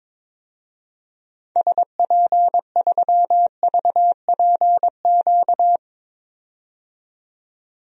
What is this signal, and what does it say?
Morse code, heard as a keyed tone, about 22 wpm, 700 Hz. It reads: SP3VPQ
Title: DXCC-100 Call Signs - 6 Characters - 22wpm